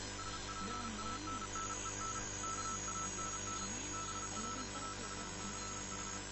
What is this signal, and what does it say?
Morse code, heard as a keyed tone, about 33 wpm, 1320 Hz, with noise. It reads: SPANNEN KANN BRAUCH